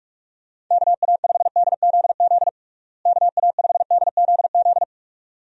Morse code, heard as a keyed tone, about 42 wpm, 700 Hz. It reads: KA5DZZ KA5DZZ
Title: USA Call Signs - 6 Characters - Repeated 2x - 45wpm